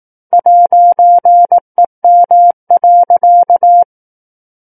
Morse code, heard as a keyed tone, about 18 wpm, 700 Hz. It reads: 'EM.
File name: 18 WPM